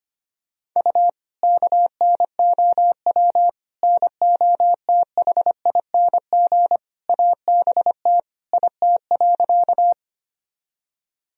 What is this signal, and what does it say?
Morse code, heard as a keyed tone, about 25 wpm, 700 Hz. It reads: U KNOW NOTHING ABT IT.
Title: Sentences from Top 500 Words and Abbreviations 25wpm